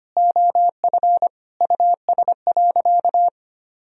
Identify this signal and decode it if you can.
Morse code, heard as a keyed tone, about 25 wpm, 700 Hz. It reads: OF US.
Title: Sentences from Top 1000 Words - Review Part 4 - 25wpm